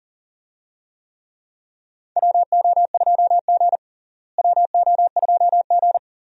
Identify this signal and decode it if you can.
Morse code, heard as a keyed tone, about 37 wpm, 700 Hz. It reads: WO2G WO2G
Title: USA Call Signs - 4 Characters - Repeated 2x - 40wpm